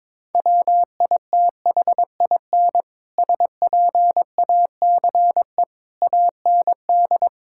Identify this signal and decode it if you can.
Morse code, heard as a keyed tone, about 22 wpm, 700 Hz. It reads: WITHIN SPACE AND
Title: Sentences from Top 1000 Words - Review Part 5 - 22wpm